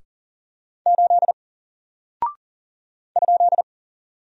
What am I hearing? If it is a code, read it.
Morse code, heard as a keyed tone, about 39 wpm, 700 Hz. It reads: 8 ?
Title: Single Character (Letter:Number:Punctuation:Prosign) Alt 40wpm